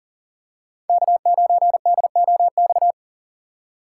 Morse code, heard as a keyed tone, about 40 wpm, 700 Hz. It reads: K9DOX
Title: USA Call Signs - 5 Characters - 3x Character Spacing - 40wpm